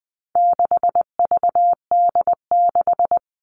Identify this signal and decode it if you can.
Morse code, heard as a keyed tone, about 20 wpm, 700 Hz. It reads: 6VD6
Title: All Cut Numbers - 4-Digit 20wpm